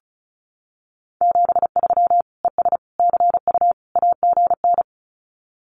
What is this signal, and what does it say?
Morse code, heard as a keyed tone, about 34 wpm, 700 Hz. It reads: 73 ES CU AGN
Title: Common QSO Phrases Encore 35wpm